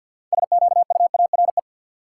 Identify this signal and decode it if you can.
Morse code, heard as a keyed tone, about 52 wpm, 700 Hz. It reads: SQUARE
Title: Sentences from Top 1000 Words - Review Part 7 - New Word Drill - 50wpm